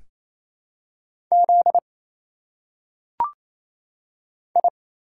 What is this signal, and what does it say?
Morse code, heard as a keyed tone, about 27 wpm, 700 Hz. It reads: Z I
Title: Single Letters - Phonetic Pronunciation 28wpm